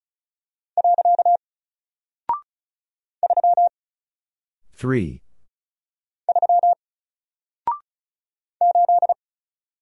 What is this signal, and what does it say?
Morse code, heard as a keyed tone, about 36 wpm, 700 Hz. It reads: . 3 3 8
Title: Single Character (Letter:Number:Punctuation:Prosign) Alt 35wpm